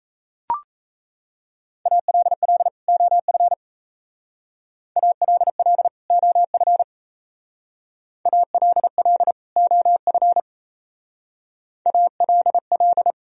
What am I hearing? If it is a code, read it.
Morse code, heard as a keyed tone, about 36 wpm, 700 Hz. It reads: ALL OF ALL OF ALL OF ALL